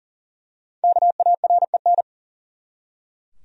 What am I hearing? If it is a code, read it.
Morse code, heard as a keyed tone, about 40 wpm, 700 Hz. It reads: KAREN
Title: USA Common Names - Max 5 Letters - 40wpm